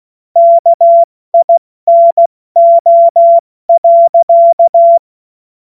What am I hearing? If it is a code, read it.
Morse code, heard as a keyed tone, about 16 wpm, 680 Hz. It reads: KINO.
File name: Morsekurs / Witze / Witz 13 Mai 2025 70 F80